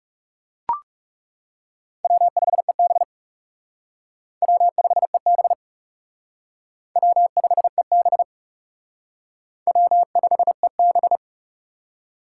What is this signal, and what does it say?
Morse code, heard as a keyed tone, about 35 wpm, 700 Hz. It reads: W5EB W5EB W5EB W5EB
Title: Speed Racing - USA Call Signs 30wpm